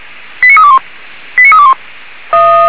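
Five-tone selective call (ZVEI), heard as quick runs of five tones, tones of about 80 ms. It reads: 89321 89321
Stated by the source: ZVEI five-tone selective call 89319 received on an SDR.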